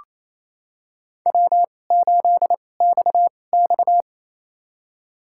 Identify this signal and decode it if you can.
Morse code, heard as a keyed tone, about 27 wpm, spaced as for 20 wpm, 700 Hz. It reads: W8XX
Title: USA Call Signs - 4 Characters - 2x Character Spacing - 28wpm